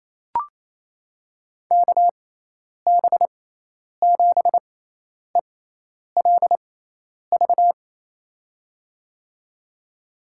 Morse code, heard as a keyed tone, about 28 wpm, spaced as for 10 wpm, 700 Hz. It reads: KB7ELV
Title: USA Call Signs - 6 Characters - 6x Character Spacing - 28wpm